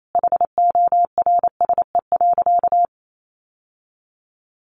Morse code, heard as a keyed tone, about 28 wpm, 700 Hz. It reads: HORSE.
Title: Sentences from Top 1100 Words 28wpm